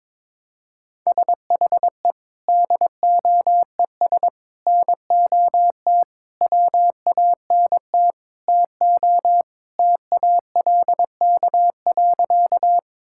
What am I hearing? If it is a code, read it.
Morse code, heard as a keyed tone, about 21 wpm, 700 Hz. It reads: SHE DOES NOT WANT TO TALK.
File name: practice / 22wpm / Sentences with 4-Word Phrases - 22wpm